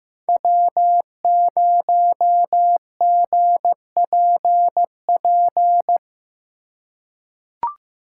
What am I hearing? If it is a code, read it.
Morse code, heard as a keyed tone, about 15 wpm, 700 Hz. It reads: W0GPP
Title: USA Call Signs - 2x Character Spacing - 15wpm